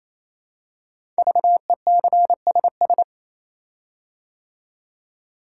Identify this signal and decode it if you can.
Morse code, heard as a keyed tone, about 27 wpm, 700 Hz. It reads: VECSS
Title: Code Groups - 5 Letters - 28wpm